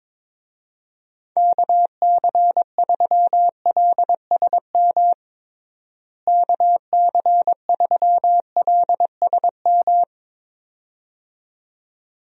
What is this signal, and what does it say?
Morse code, heard as a keyed tone, about 21 wpm, 700 Hz. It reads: KC3LSM KC3LSM
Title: USA Call Signs - 6 Characters - Repeated 2x - 22wpm